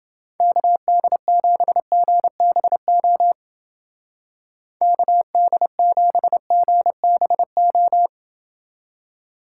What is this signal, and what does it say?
Morse code, heard as a keyed tone, about 28 wpm, 700 Hz. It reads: KD7GBO KD7GBO